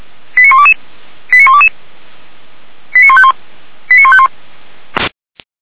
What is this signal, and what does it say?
Five-tone selective call (ZVEI), heard as quick runs of five tones, tones of about 70 ms. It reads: 89122 89122 89151 89151